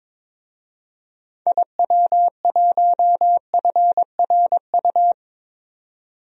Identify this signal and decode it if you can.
Morse code, heard as a keyed tone, about 22 wpm, 700 Hz. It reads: IW1FRU
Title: DXCC-100 Call Signs - 6 Characters - 22wpm